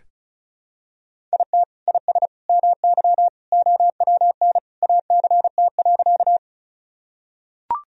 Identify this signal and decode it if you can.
Morse code, heard as a keyed tone, about 36 wpm, 700 Hz. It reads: IT IS MY OWN ACT.
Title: Sentences from Top 1000 Words - Review Part 1 - 35wpm